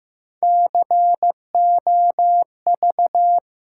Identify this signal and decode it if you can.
Morse code, heard as a keyed tone, about 15 wpm, 700 Hz. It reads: COV
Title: Sentences from Top 500 Words - New Word Drill - 15wpm